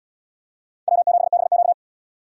Morse code, heard as a keyed tone, about 77 wpm, 690 Hz. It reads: W7BQ